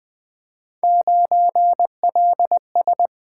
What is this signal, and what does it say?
Morse code, heard as a keyed tone, about 20 wpm, 700 Hz. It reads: 9LS
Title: Three Letter-Number Combinations - 20wpm